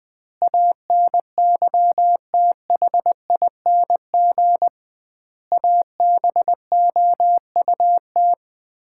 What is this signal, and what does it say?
Morse code, heard as a keyed tone, about 20 wpm, 700 Hz. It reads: ANYTHING ABOUT
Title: 2x Word-Spacing - Sets of 5 Words - 20wpm